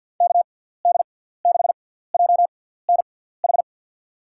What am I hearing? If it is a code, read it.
Morse code, heard as a keyed tone, about 53 wpm, spaced as for 17 wpm, 700 Hz. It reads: KD6JNH
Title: USA Call Signs - 6x Character Spacing - 50wpm